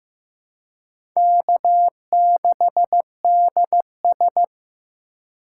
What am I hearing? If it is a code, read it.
Morse code, heard as a keyed tone, about 15 wpm, 700 Hz. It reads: K6DS
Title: USA Call Signs - 4 Characters - 2x Character Spacing - 15wpm